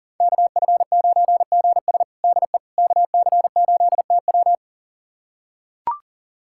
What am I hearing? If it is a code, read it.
Morse code, heard as a keyed tone, about 37 wpm, 700 Hz. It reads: KF9GS DE KC8TW
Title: USA Callsign DE Callsign - 5 Characters - 40wpm